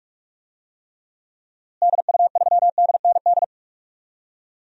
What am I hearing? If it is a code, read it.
Morse code, heard as a keyed tone, about 46 wpm, 700 Hz. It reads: DU3DND
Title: DXCC-100 Call Signs - 6 Characters - Repeated 2x - 45wpm